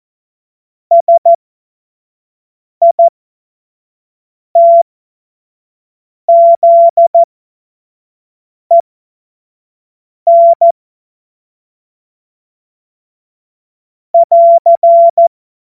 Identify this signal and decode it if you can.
Morse code, heard as a keyed tone, about 14 wpm, spaced as for 5 wpm, 680 Hz. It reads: SITZEN <AR>
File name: Morsekurs / Witze / Witz 19 Dezember 2023 25 F70